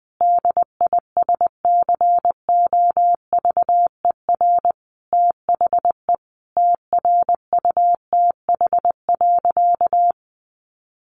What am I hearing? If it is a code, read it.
Morse code, heard as a keyed tone, about 20 wpm, 700 Hz. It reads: DISCOVER THE TRUTH.
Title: Sentences from Top 1100 Words 20wpm